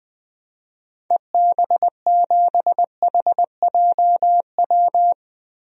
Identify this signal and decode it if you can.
Morse code, heard as a keyed tone, about 20 wpm, 700 Hz. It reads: EB7HJW
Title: DXCC-100 Call Signs - 6 Characters - Repeated 2x - 20wpm